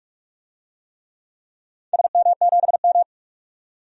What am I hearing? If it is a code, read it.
Morse code, heard as a keyed tone, about 46 wpm, 700 Hz. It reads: SM7M
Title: DXCC-100 Call Signs - 4 Characters - 45wpm